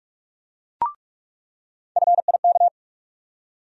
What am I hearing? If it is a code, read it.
Morse code, heard as a keyed tone, about 43 wpm, 700 Hz. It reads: FIK
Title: USA Call Sign Postfixes - 3 Letters 45wpm